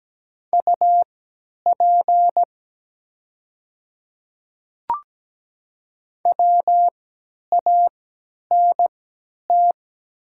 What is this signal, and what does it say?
Morse code, heard as a keyed tone, about 17 wpm, spaced as for 10 wpm, 700 Hz. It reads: UP WANT
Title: 3x Character Spacing - Top 200 Words - 17wpm